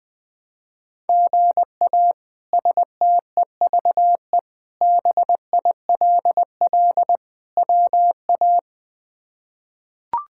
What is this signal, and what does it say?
Morse code, heard as a keyed tone, about 20 wpm, 700 Hz. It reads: GA STEVE BILL WA
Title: K1USN SST QSO Practice - 20wpm